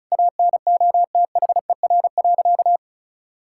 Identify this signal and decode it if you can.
Morse code, heard as a keyed tone, about 36 wpm, 700 Hz. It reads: ANOTHER.
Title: Sentences from Top 1600 Words 35wpm